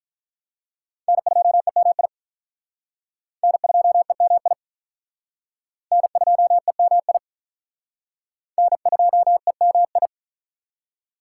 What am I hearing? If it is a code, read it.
Morse code, heard as a keyed tone, about 44 wpm, 700 Hz. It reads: N2EMI N2EMI N2EMI N2EMI